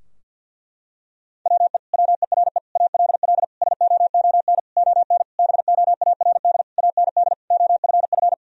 Morse code, heard as a keyed tone, about 50 wpm, 700 Hz. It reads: WE WERE ALL SOON ON BOARD AND OFF
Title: Sentences from Top 1000 Words - Review Part 4 - 50wpm